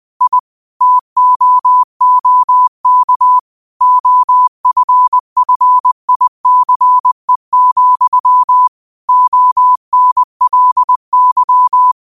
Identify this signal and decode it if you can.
Morse code, heard as a keyed tone, about 20 wpm, 1000 Hz. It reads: I TOOK OFFICE, ONLY